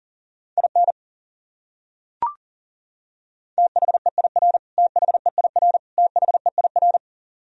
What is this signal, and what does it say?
Morse code, heard as a keyed tone, about 37 wpm, 700 Hz. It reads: IN THEIR THEIR THEIR
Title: Top 100 Words Repeated 40wpm